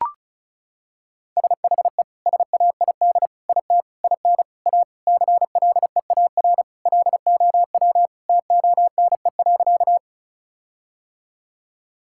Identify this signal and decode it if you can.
Morse code, heard as a keyed tone, about 36 wpm, 700 Hz. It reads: SHE SAID IT IN A CLEAR LOW TONE.
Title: Sentences from Top 1600 Words 35wpm